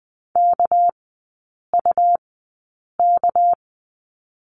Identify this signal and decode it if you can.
Morse code, heard as a keyed tone, about 19 wpm, 700 Hz. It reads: K U K
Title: 198 Lesson 32e - Letter k - Three Characters 20wpm